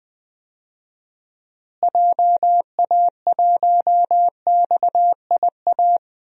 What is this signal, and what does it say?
Morse code, heard as a keyed tone, about 20 wpm, 700 Hz. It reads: JA1XIA
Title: DXCC-100 Call Signs - 6 Characters - Repeated 2x - 20wpm